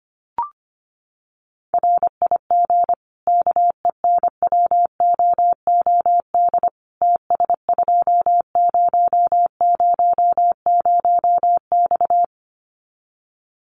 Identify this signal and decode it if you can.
Morse code, heard as a keyed tone, about 25 wpm, 700 Hz. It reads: RIG KENWOOD TS2000X
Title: Common QSO Phrases 25wpm